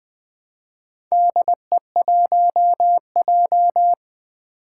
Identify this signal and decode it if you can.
Morse code, heard as a keyed tone, about 20 wpm, 700 Hz. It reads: DE1J
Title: Four Challenging Letter-Number Combinations 20wpm